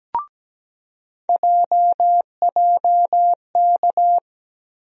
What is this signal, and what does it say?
Morse code, heard as a keyed tone, about 17 wpm, 700 Hz. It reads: JJK
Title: USA Call Sign Postfixes - 3 Letters 17wpm